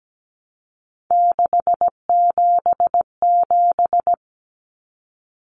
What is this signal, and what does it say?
Morse code, heard as a keyed tone, about 17 wpm, 700 Hz. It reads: 677
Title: Numbers - 3-Digit 17wpm